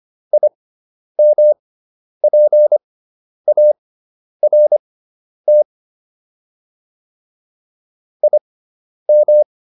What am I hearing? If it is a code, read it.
Morse code, heard as a keyed tone, about 25 wpm, spaced as for 10 wpm, 600 Hz. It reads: IMPART IM